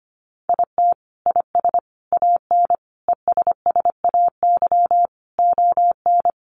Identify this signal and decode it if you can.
Morse code, heard as a keyed tone, about 25 wpm, 700 Hz. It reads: IT IS AN ESSAY ON
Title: Sentences from Top 1500 Words 25wpm